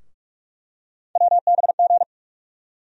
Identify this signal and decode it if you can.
Morse code, heard as a keyed tone, about 42 wpm, 700 Hz. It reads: WBG
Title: USA Call Sign Postfixes - 3 Letters 45wpm